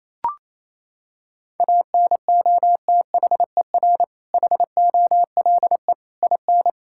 Morse code, heard as a keyed tone, about 28 wpm, 700 Hz. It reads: ANOTHER HOLE IN
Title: Sentences from Top 1600 Words 28wpm